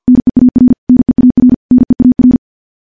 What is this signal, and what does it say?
Morse code, heard as a keyed tone, about 50 wpm, 250 Hz. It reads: NEAR NEAR NEAR
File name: morse-training-50WPM